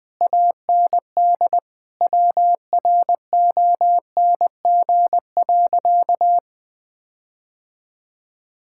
Morse code, heard as a keyed tone, about 20 wpm, 700 Hz. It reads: AND WRONG.
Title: Sentences from Top 1600 Words 20wpm